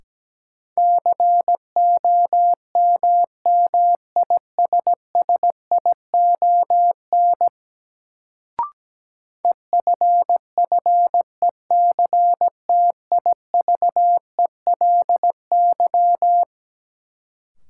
Morse code, heard as a keyed tone, about 17 wpm, 700 Hz. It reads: COMMISSION EFFECTIVELY
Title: Sentences from Top 1400 Words - New Word Drill - 17wpm